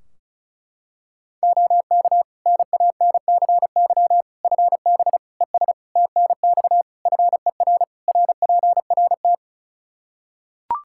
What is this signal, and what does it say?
Morse code, heard as a keyed tone, about 36 wpm, 700 Hz. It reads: OK NANCY FB ES TNX FER RPRT
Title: Practice QSO - Exchange 2 - Info - 35wpm